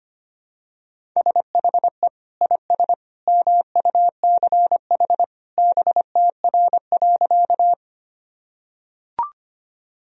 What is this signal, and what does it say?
Morse code, heard as a keyed tone, about 25 wpm, 700 Hz. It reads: SHE IS MUCH BTR.
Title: Sentences from Top 400 Words and Abbreviations 25wpm